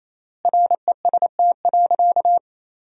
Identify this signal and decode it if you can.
Morse code, heard as a keyed tone, about 27 wpm, 700 Hz. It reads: REST.
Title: 3x Word-Spacing - Sentences from Top 500 Words - 28wpm